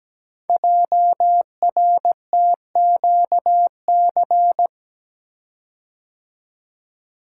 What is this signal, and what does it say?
Morse code, heard as a keyed tone, about 17 wpm, 700 Hz. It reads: JRTQC
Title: Code Groups - 5 Letters - 17wpm